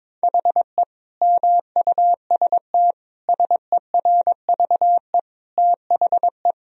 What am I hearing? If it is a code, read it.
Morse code, heard as a keyed tone, about 23 wpm, 700 Hz. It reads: HE MUST SERVE THE
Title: Sentences from Top 1000 Words - Review Part 2 - 22wpm